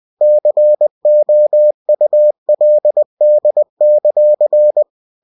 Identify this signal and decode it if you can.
Morse code, heard as a keyed tone, about 20 wpm, 600 Hz. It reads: COULD;
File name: cw-books / winnie-the-pooh / winnie-20wpm-page-0048